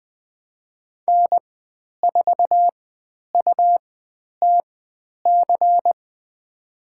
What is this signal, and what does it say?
Morse code, heard as a keyed tone, about 20 wpm, spaced as for 10 wpm, 700 Hz. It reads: N4UTC
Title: 109 Lesson 18g - Letter m - Call Signs 20wpm